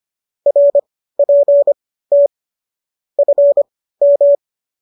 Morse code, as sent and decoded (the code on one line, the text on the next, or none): .-. .--. - / ..-. --
RPT FM